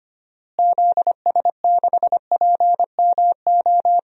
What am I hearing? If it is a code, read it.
Morse code, heard as a keyed tone, about 25 wpm, 700 Hz. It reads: ZS6PMO
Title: DXCC-100 Call Signs - 6 Characters - Repeated 2x - 25wpm